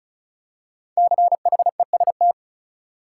Morse code, heard as a keyed tone, about 34 wpm, 700 Hz. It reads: CHEST
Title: Five Letter Words - 35wpm